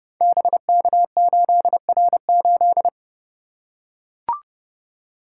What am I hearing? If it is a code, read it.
Morse code, heard as a keyed tone, about 30 wpm, 700 Hz. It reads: BK8R8
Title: Code Groups - 5 Letter:Numbers - 30wpm